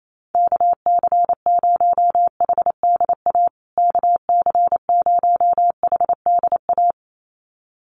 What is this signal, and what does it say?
Morse code, heard as a keyed tone, about 28 wpm, 700 Hz. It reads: KC0HDA KC0HDA
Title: USA Call Signs Repeated 2x 28wpm